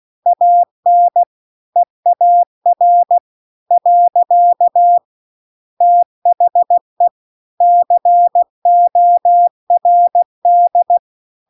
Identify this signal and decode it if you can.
Morse code, heard as a keyed tone, about 16 wpm, 700 Hz. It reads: AN EAR . THE CORD